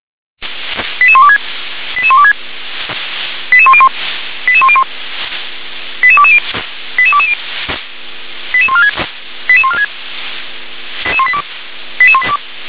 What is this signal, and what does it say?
Five-tone selective call (ZVEI), heard as quick runs of five tones, tones of about 70 ms. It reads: 88126 88126 88101 88101 88220 88220 88146 88146 88182
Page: /357/SDR/ZVEI/